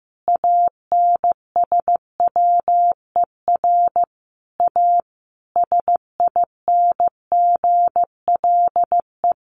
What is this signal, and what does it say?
Morse code, heard as a keyed tone, about 14 wpm, 700 Hz. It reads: ANSWER A SINGLE
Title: Sentences from Top 700 Words 15wpm